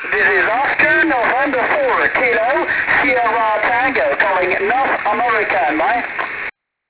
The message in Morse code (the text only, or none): none